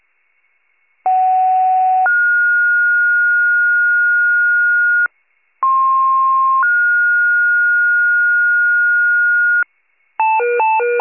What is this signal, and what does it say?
Two-tone page, heard as a one-second tone, then a three-second tone, about 750 then 1430 Hz, about 1060 then 1430 Hz. It is heard from 1.1 to 9.6 s.
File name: wcalert